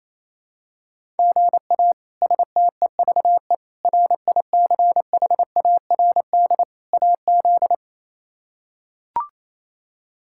Morse code, heard as a keyed tone, about 28 wpm, 700 Hz. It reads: GA STEVE RICHARD AZ
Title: K1USN SST QSO Practice - 28wpm